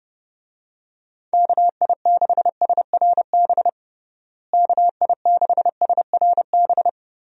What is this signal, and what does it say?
Morse code, heard as a keyed tone, about 29 wpm, 700 Hz. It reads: KI6SRB KI6SRB
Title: USA Call Signs - 6 Characters - Repeated 3x - 30wpm